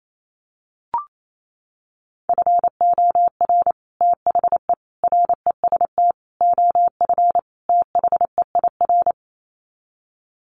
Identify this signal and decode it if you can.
Morse code, heard as a keyed tone, about 28 wpm, 700 Hz. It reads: FOR THE REST OF THEIR
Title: Sets of 5 Words Encore2 28wpm